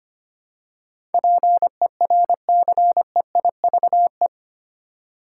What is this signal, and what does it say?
Morse code, heard as a keyed tone, about 25 wpm, 700 Hz. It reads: PERCEIVE
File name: 8 Letter Words - First 3 Letters Given - 25wpm